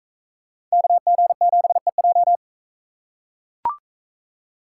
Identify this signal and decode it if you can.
Morse code, heard as a keyed tone, about 42 wpm, 700 Hz. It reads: KG7EJ